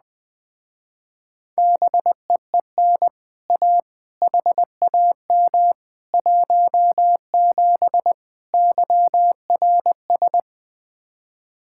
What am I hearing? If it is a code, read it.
Morse code, heard as a keyed tone, about 20 wpm, 700 Hz. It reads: BEEN A HAM 17 YRS
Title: Speed-Racing Common QSO Phrases Encore - 17wpm